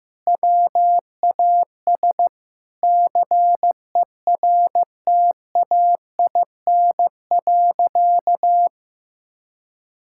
Morse code, heard as a keyed tone, about 15 wpm, 700 Hz. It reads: WAS CERTAIN.
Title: Sentences from Top 1000 Words - Review Part 6 - 15wpm